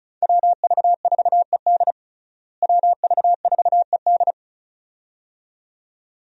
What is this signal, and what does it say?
Morse code, heard as a keyed tone, about 36 wpm, 700 Hz. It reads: WV4ED WV4ED
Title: USA Call Signs - 5 Characters - Repeated 2x - 35wpm